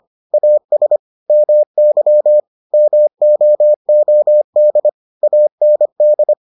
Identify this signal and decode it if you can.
Morse code, heard as a keyed tone, about 25 wpm, 600 Hz. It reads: AS MY MOOD AND